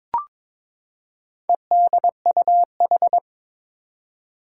Morse code, heard as a keyed tone, about 22 wpm, 700 Hz. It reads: EDUH
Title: Four Challenging Letter-Number Combinations 22wpm